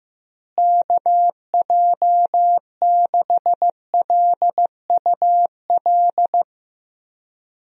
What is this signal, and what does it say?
Morse code, heard as a keyed tone, about 16 wpm, 700 Hz. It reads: KJ6LUL
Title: USA Call Signs - 6 Characters - Repeated 2x - 15wpm